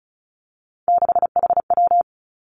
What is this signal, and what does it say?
Morse code, heard as a keyed tone, about 34 wpm, 700 Hz. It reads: 6HW
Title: Three Challenging Letter-Number Combinations 35wpm